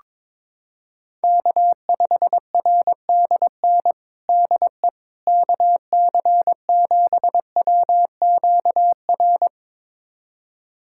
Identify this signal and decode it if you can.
Morse code, heard as a keyed tone, about 22 wpm, 700 Hz. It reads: K5RDN DE KC7WQR
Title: USA Callsign DE Callsign 22wpm